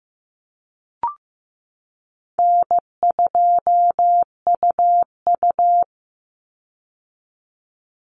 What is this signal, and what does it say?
Morse code, heard as a keyed tone, about 15 wpm, 700 Hz. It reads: N2UU
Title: Call Signs Repeated 2x - 15wpm